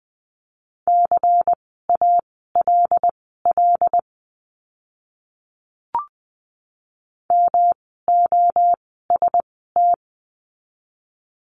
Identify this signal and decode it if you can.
Morse code, heard as a keyed tone, about 20 wpm, spaced as for 14 wpm, 700 Hz. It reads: CALL MOST
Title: 139 Lesson 23f - Letter f - Words 20wpm